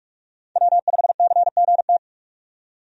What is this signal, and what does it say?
Morse code, heard as a keyed tone, about 42 wpm, 700 Hz. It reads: W5CGT
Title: USA Call Signs - 5 Characters - 45wpm